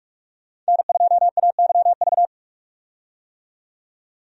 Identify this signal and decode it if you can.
Morse code, heard as a keyed tone, about 46 wpm, 700 Hz. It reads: N2AYV
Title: USA Call Signs - 5 Characters - 45wpm